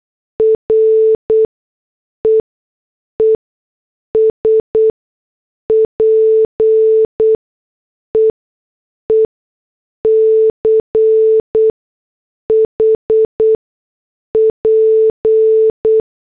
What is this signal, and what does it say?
Morse code, heard as a keyed tone, about 8 wpm, spaced as for 6 wpm, 440 Hz. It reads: REE SPEECH P